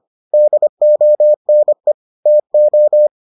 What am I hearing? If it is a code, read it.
Morse code, heard as a keyed tone, about 25 wpm, 600 Hz. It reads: DONE TO